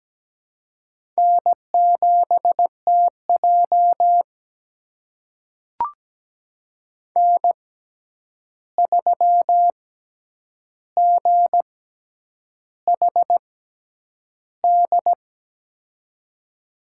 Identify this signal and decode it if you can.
Morse code, heard as a keyed tone, about 17 wpm, spaced as for 8 wpm, 700 Hz. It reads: N7TJ N3GHD
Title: USA Call Signs - 6x Character Spacing - 17wpm